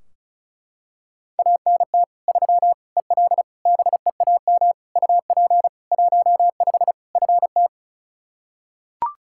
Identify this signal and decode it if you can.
Morse code, heard as a keyed tone, about 36 wpm, 700 Hz. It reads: ANT 3 EL BEAM UP 15 FT
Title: K1USN SST QSO Practice 35wpm